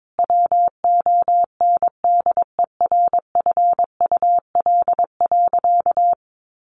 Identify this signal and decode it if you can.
Morse code, heard as a keyed tone, about 22 wpm, 700 Hz. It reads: WONDERFUL.
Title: Sentences from Top 1100 Words 22wpm